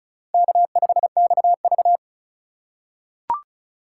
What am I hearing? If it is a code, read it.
Morse code, heard as a keyed tone, about 34 wpm, 700 Hz. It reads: K5XV
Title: USA Call Signs - 4 Characters - 3x Character Spacing - 35wpm